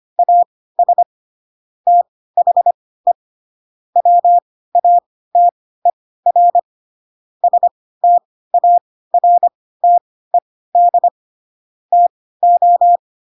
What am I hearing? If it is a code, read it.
Morse code, heard as a keyed tone, about 25 wpm, spaced as for 16 wpm, 700 Hz. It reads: AS THE WATER STARTED TO